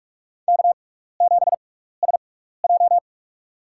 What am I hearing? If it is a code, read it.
Morse code, heard as a keyed tone, about 45 wpm, spaced as for 15 wpm, 700 Hz. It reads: K7SJ
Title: USA Call Signs - 4 Characters - 6x Character Spacing - 45wpm